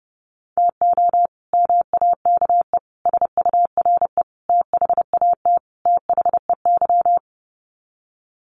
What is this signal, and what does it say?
Morse code, heard as a keyed tone, about 30 wpm, 700 Hz. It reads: TO MAKE SURE THAT THEY
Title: Sets of 5 Words Encore2 30wpm